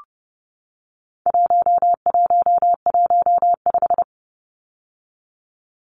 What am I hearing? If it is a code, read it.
Morse code, heard as a keyed tone, about 30 wpm, 700 Hz. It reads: 1115
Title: Numbers - 4-Digit 30wpm